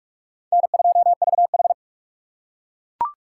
Morse code, heard as a keyed tone, about 41 wpm, 700 Hz. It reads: N2VH
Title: USA Call Signs - 4 Characters - 3x Character Spacing - 45wpm